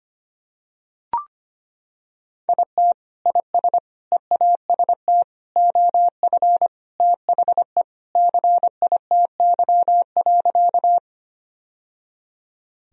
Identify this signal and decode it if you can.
Morse code, heard as a keyed tone, about 25 wpm, 700 Hz. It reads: IT IS EAST OF THE CITY.
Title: Sentences from Top 1000 Words - Review Part 3 - 25wpm